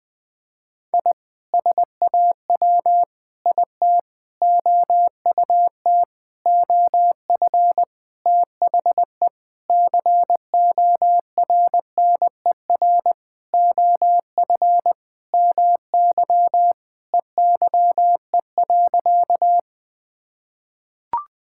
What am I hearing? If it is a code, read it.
Morse code, heard as a keyed tone, about 20 wpm, 700 Hz. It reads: I SAW IT OUT OF THE CORNER OF MY EYE.
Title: Sentences with 7-Word Phrases - Spoken Beginning - 20wpm